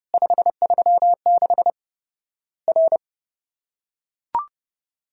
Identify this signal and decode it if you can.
Morse code, heard as a keyed tone, about 30 wpm, 700 Hz. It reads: 536 R
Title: Worked All Europe DX Contest - QTC Exchange - 30wpm